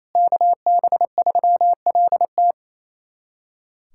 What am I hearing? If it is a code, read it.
Morse code, heard as a keyed tone, about 28 wpm, 700 Hz. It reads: KB3LT
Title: USA Callsign DE Callsign 28wpm